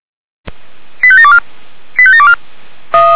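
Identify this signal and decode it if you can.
Five-tone selective call (ZVEI), heard as quick runs of five tones, tones of about 70 ms. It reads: 85724 85724
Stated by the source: ZVEI five-tone selective call received on SDR.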